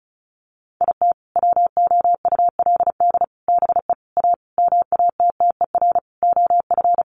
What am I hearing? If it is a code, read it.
Morse code, heard as a keyed tone, about 34 wpm, 700 Hz. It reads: IT WOULD BE A MATTER OF
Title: Sentences from Top 1400 Words 35wpm